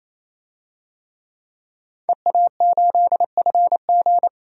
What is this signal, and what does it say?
Morse code, heard as a keyed tone, about 28 wpm, 700 Hz. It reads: EA8FG
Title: DXCC-100 Call Signs - 5 Characters - 28wpm